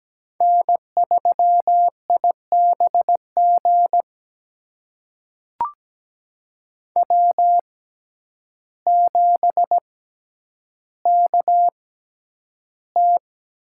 Morse code, heard as a keyed tone, about 17 wpm, spaced as for 9 wpm, 700 Hz. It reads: N3IBG W7KT
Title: USA Call Signs - 5 Characters - 6x Character Spacing - 17wpm